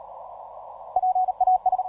Morse code, heard as a keyed tone, about 38 wpm, 740 Hz. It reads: PAS